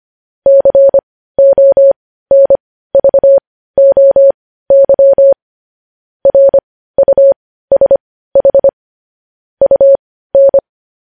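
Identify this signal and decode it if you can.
Morse code, heard as a keyed tone, about 25 wpm, spaced as for 15 wpm, 550 Hz. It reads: CONVOY RUSH UN
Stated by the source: News headlines sent in Morse code at 15 words per minute, preceded by vvv.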